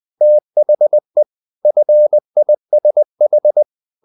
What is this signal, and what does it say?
Morse code, heard as a keyed tone, about 20 wpm, 600 Hz. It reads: THE FISH
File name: the old man and the sea / the old man and the sea part 1 20wpm